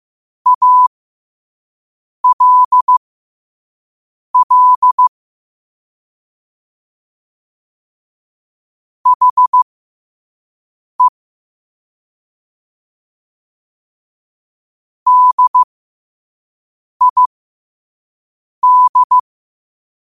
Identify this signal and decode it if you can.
Morse code, heard as a keyed tone, about 15 wpm, spaced as for 5 wpm, 1000 Hz. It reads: ALL HE DID
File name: qotdfiles / 5wpm